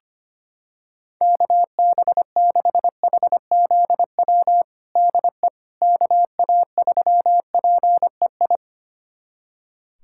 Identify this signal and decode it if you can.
Morse code, heard as a keyed tone, about 25 wpm, 700 Hz. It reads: KB6HZW DE KA3PEI
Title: USA Callsign DE Callsign - 6 Characters - 25wpm